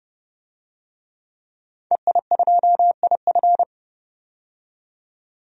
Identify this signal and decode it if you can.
Morse code, heard as a keyed tone, about 31 wpm, 700 Hz. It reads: EI2IF
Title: DXCC-100 Call Signs - 5 Characters - 30wpm